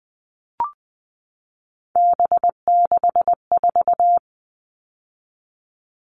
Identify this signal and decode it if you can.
Morse code, heard as a keyed tone, about 20 wpm, 700 Hz. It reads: B64
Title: All Cut Numbers - 3-Digit 20wpm